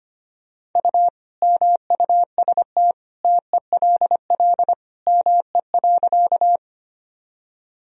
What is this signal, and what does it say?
Morse code, heard as a keyed tone, about 25 wpm, 700 Hz. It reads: U MUST TELL ME.
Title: Sentences from Top 200 Words and Abbreviations 25wpm